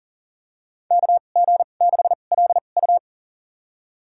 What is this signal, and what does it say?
Morse code, heard as a keyed tone, about 40 wpm, spaced as for 29 wpm, 700 Hz. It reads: KG6LU
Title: USA Call Signs - 5 Characters - 2x Character Spacing - 40wpm